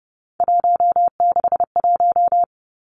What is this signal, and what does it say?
Morse code, heard as a keyed tone, about 30 wpm, 700 Hz. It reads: 161
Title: Three Challenging Letter-Number Combinations 30wpm